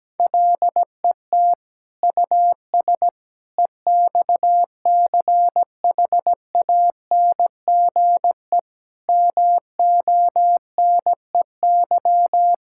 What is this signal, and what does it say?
Morse code, heard as a keyed tone, about 17 wpm, 700 Hz. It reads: LET US EXCHANGE MONEY
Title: Sentences from Top 1000 Words - Review Part 6 - 17wpm